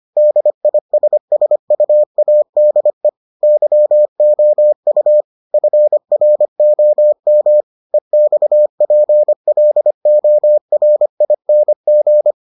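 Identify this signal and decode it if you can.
Morse code, heard as a keyed tone, about 25 wpm, 600 Hz. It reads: DISSUADE YOU FROM EXPLORING
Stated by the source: Morse edition of the Foundations of Amateur Radio episodes.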